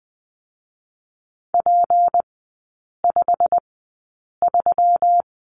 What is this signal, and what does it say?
Morse code, heard as a keyed tone, about 20 wpm, spaced as for 8 wpm, 700 Hz. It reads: P53
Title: 152 Lesson 25e - Letter p - Three Characters 20wpm